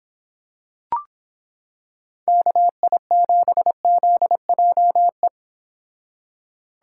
Morse code, heard as a keyed tone, about 25 wpm, 700 Hz. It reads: KI7ZJE